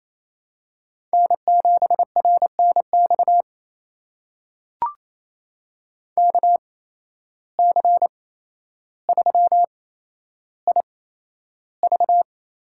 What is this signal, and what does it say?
Morse code, heard as a keyed tone, about 28 wpm, spaced as for 12 wpm, 700 Hz. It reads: N7RNX KC3IV
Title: USA Call Signs - 8x Character Spacing - 28wpm